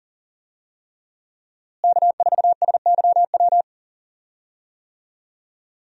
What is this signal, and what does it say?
Morse code, heard as a keyed tone, about 40 wpm, 700 Hz. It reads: K4SYW
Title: USA Call Signs - 5 Characters - Repeated 3x - 40wpm